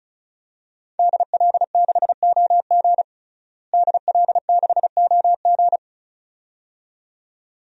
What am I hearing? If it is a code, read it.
Morse code, heard as a keyed tone, about 34 wpm, 700 Hz. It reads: DL6OG DL6OG
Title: DXCC-100 Call Signs - 5 Characters - Repeated 2x - 35wpm